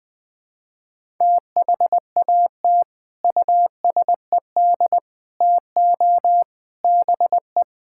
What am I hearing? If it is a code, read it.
Morse code, heard as a keyed tone, about 20 wpm, 700 Hz. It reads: THAT USED TO BE
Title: Sentences with 3-Word Phrases - 20wpm